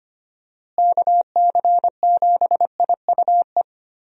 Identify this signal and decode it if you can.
Morse code, heard as a keyed tone, about 25 wpm, 700 Hz. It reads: KC7IUE
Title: USA Call Signs - 6 Characters - Repeated 3x - 25wpm